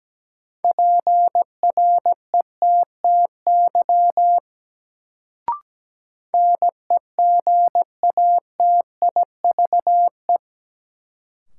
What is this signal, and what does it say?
Morse code, heard as a keyed tone, about 17 wpm, 700 Hz. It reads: PRETTY NEGATIVE
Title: Sentences from Top 1000 Words - New Word Drill - 17wpm